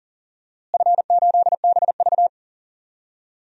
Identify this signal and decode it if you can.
Morse code, heard as a keyed tone, about 40 wpm, 700 Hz. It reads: F8BV
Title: DXCC-100 Call Signs - 4 Characters - Repeated 3x - 40wpm